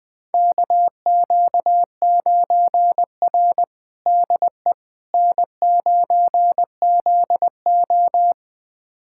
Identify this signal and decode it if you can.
Morse code, heard as a keyed tone, about 20 wpm, 700 Hz. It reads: KQ9R DE N9ZO
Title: USA Callsign DE Callsign - 4 Characters - 20wpm